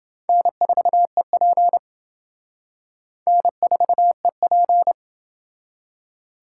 Morse code, heard as a keyed tone, about 27 wpm, 700 Hz. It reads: N4EP N4EP